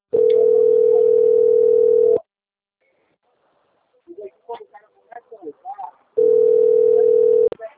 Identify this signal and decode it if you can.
Ringback tone (North America), 440 + 480 Hz, with speech.
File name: out- - - - - .36187